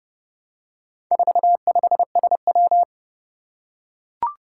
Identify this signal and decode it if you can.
Morse code, heard as a keyed tone, about 29 wpm, 700 Hz. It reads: 45SW
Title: Four Challenging Letter-Number Combinations 30wpm